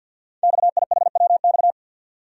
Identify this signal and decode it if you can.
Morse code, heard as a keyed tone, about 47 wpm, 700 Hz. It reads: XIHWX
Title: Code Groups - 5 Letters - 50wpm